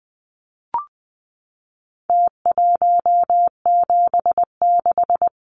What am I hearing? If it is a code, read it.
Morse code, heard as a keyed tone, about 20 wpm, 700 Hz. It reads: T176
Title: Numbers - Contest Style - 4-Digit 20wpm